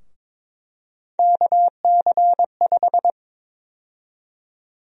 Morse code, heard as a keyed tone, about 21 wpm, 700 Hz. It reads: KC5
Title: USA Call Sign Prefixes - 22wpm